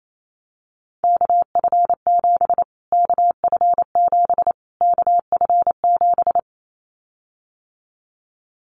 Morse code, heard as a keyed tone, about 28 wpm, 700 Hz. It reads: KF7 KF7 KF7
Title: Callsign Prefixes for Top 25 DXCC - Repeated 3x - 28wpm